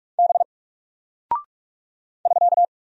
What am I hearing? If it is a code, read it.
Morse code, heard as a keyed tone, about 48 wpm, 700 Hz. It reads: B <SK>
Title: Single Character (Letter:Number:Punctuation:Prosign) Alt 45wpm